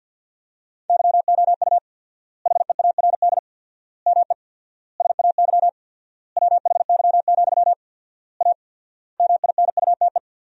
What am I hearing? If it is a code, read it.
Morse code, heard as a keyed tone, about 48 wpm, 700 Hz. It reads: YOU HEARD ME SAY WHY, A MINUTE
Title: 4x Word-Spacing - Sentences from Top 500 Words - 50wpm